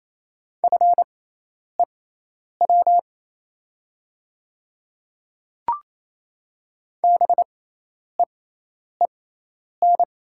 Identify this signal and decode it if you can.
Morse code, heard as a keyed tone, about 28 wpm, spaced as for 10 wpm, 700 Hz. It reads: FEW BEEN
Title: 6x Character Spacing - Top 300 Words - 28wpm